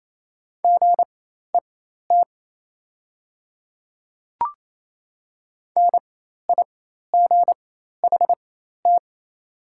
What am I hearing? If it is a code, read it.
Morse code, heard as a keyed tone, about 28 wpm, spaced as for 13 wpm, 700 Hz. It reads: GET NIGHT
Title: 4x Character Spacing - Top 300 Words - 28wpm